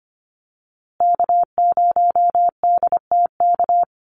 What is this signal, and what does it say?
Morse code, heard as a keyed tone, about 25 wpm, 700 Hz. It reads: K0DTK
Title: USA Call Signs Repeated 3x 25wpm